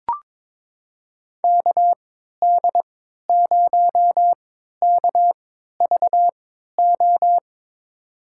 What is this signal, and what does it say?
Morse code, heard as a keyed tone, about 21 wpm, spaced as for 12 wpm, 700 Hz. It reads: KD0KVO
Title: USA Call Signs - 6 Characters - 3x Character Spacing - 22wpm